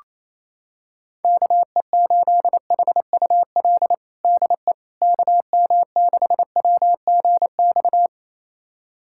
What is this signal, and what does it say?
Morse code, heard as a keyed tone, about 28 wpm, 700 Hz. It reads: KE8HUL DE KM6WGX
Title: USA Callsign DE Callsign 28wpm